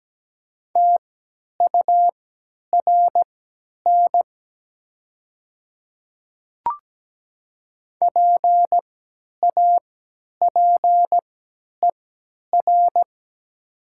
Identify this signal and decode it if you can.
Morse code, heard as a keyed tone, about 17 wpm, spaced as for 10 wpm, 700 Hz. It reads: TURN PAPER
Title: 3x Character Spacing - Top 300 Words - 17wpm